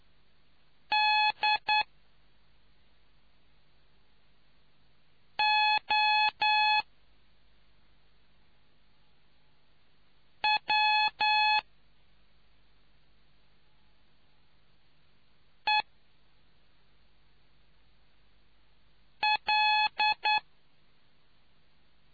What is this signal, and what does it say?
Morse code, heard as a keyed tone, about 9 wpm, spaced as for 2 wpm, 850 Hz. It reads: DOWEL